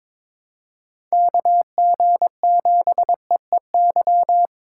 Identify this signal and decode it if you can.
Morse code, heard as a keyed tone, about 22 wpm, 700 Hz. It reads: KG7EEY
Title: USA Call Signs - 3x Character Spacing - 22wpm